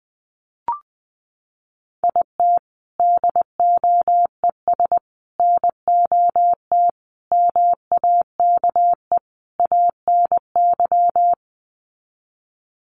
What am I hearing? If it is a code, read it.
Morse code, heard as a keyed tone, about 20 wpm, 700 Hz. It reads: IT DOES NOT MAKE ANY
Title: Sets of 5 Words Encore2 20wpm